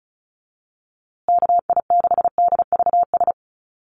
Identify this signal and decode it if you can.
Morse code, heard as a keyed tone, about 34 wpm, 700 Hz. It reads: KI6DVS
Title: USA Call Signs Repeated 2x 35wpm